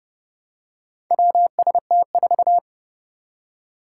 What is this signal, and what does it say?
Morse code, heard as a keyed tone, about 29 wpm, 700 Hz. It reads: WST4
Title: Four Challenging Letter-Number Combinations 30wpm